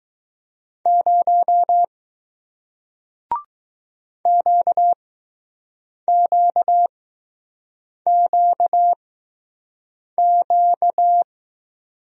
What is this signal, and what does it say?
Morse code, heard as a keyed tone, about 20 wpm, spaced as for 6 wpm, 700 Hz. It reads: 0 QQQQ